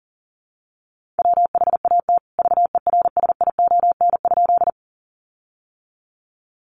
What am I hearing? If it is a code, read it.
Morse code, heard as a keyed tone, about 40 wpm, 700 Hz. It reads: WHAT VERSION?
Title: Sentences from Top 1400 Words 40wpm